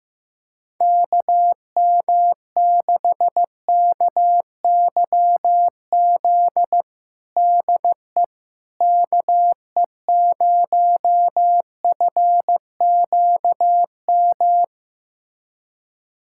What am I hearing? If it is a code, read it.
Morse code, heard as a keyed tone, about 15 wpm, 700 Hz. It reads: KM6KYZ DE KE0FQM
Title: USA Callsign DE Callsign 15wpm